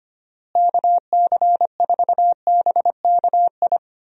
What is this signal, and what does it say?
Morse code, heard as a keyed tone, about 25 wpm, 700 Hz. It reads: KC4BKI
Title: USA Call Signs - 6 Characters - Repeated 3x - 25wpm